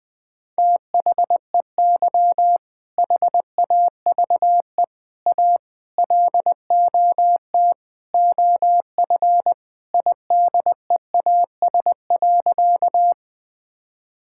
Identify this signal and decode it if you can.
Morse code, heard as a keyed tone, about 21 wpm, 700 Hz. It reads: THEY HAVE A LOT OF IDEAS.
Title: Sentences with 4-Word Phrases - 20wpm